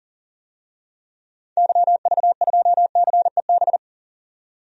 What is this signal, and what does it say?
Morse code, heard as a keyed tone, about 38 wpm, 700 Hz. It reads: YV2CEB
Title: DXCC-100 Call Signs - 6 Characters - 40wpm